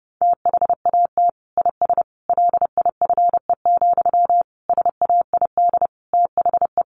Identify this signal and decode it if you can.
Morse code, heard as a keyed tone, about 30 wpm, 700 Hz. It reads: THAT IS LIFE, SAID THE
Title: Sentences from Top 900 Words 30wpm